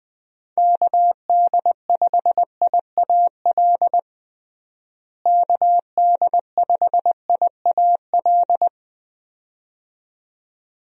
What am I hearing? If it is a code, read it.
Morse code, heard as a keyed tone, about 20 wpm, 700 Hz. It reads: KD5IAL KD5IAL
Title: USA Call Signs - 6 Characters - Repeated 2x - 20wpm